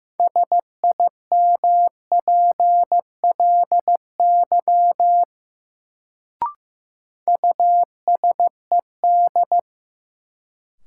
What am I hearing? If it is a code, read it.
Morse code, heard as a keyed tone, about 15 wpm, 700 Hz. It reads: SIMPLY USED
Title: Sentences from Top 600 Words - New Word Drill - 15wpm